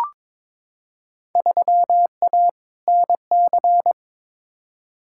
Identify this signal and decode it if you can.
Morse code, heard as a keyed tone, about 22 wpm, 700 Hz. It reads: 3A NC
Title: Field Day Exchanges - 22wpm